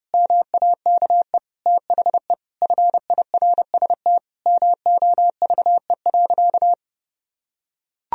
Morse code, heard as a keyed tone, about 30 wpm, 700 Hz. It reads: MAKE THE FIRST MOVE.
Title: Sentences from Top 1000 Words - Review Part 1 - 30wpm